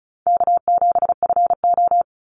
Morse code, heard as a keyed tone, about 34 wpm, 700 Hz. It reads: K7FO
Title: NA SOTA Chaser Callsigns 35wpm